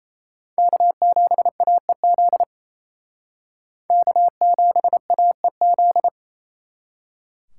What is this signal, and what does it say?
Morse code, heard as a keyed tone, about 35 wpm, 700 Hz. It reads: K7AEZ K7AEZ